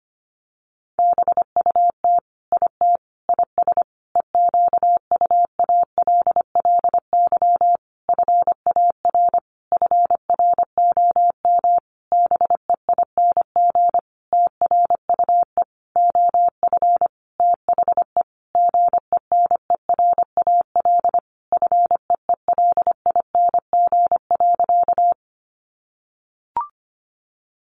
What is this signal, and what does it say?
Morse code, heard as a keyed tone, about 25 wpm, 700 Hz. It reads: BUT IT IS EQUALLY FAR FROM BEING TRUE OF THE GENERAL FEELING.
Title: Sentences from Top 1400 Words 25wpm